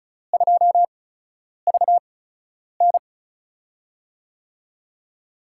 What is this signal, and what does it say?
Morse code, heard as a keyed tone, about 36 wpm, spaced as for 10 wpm, 700 Hz. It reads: 2VN
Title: USA Call Signs - 8x Character Spacing - 35wpm